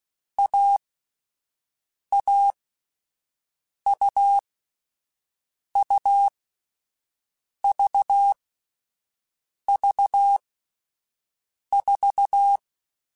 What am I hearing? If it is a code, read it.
Morse code, heard as a keyed tone, about 16 wpm, 800 Hz. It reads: A A U U V V 4
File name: Difficult Combos / Difficult Combos-mp3s / AUV AUV4 @10